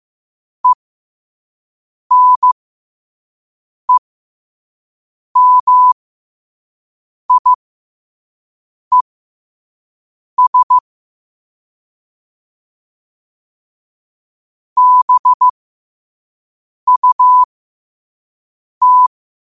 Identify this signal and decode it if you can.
Morse code, heard as a keyed tone, about 15 wpm, spaced as for 5 wpm, 1000 Hz. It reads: ENEMIES BUT
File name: qotdfiles / 5wpm